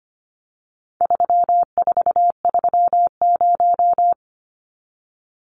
Morse code, heard as a keyed tone, about 25 wpm, 700 Hz. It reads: 3430
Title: Numbers - 4-Digit 25wpm